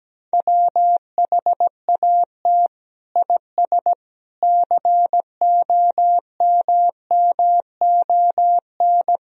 Morse code, heard as a keyed tone, about 17 wpm, 700 Hz. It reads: WHAT IS COMMON
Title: Sentences from Top 1000 Words - Review Part 3 - 17wpm